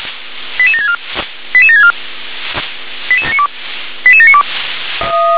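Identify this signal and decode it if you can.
Five-tone selective call (ZVEI), heard as quick runs of five tones, tones of about 70 ms. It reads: 88764 88764 88792
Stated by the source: Five-tone ZVEI selective call received by SDR.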